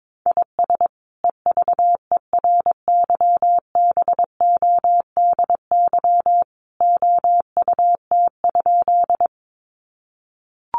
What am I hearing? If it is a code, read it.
Morse code, heard as a keyed tone, about 22 wpm, 700 Hz. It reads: IS EVERYBODY OUT?
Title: Sentences from Top 1400 Words 22wpm